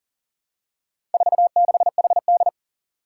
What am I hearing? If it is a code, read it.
Morse code, heard as a keyed tone, about 40 wpm, 700 Hz. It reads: 46HD
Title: Four Challenging Letter-Number Combinations 40wpm